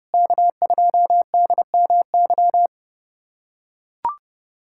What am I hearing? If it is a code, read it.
Morse code, heard as a keyed tone, about 30 wpm, 700 Hz. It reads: K2DMY
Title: USA Callsign DE Callsign - 5 Characters - 30wpm